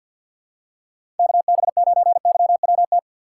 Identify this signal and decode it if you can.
Morse code, heard as a keyed tone, about 51 wpm, 700 Hz. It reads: KB9YWT
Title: USA Call Signs - 2x Character Spacing - 50wpm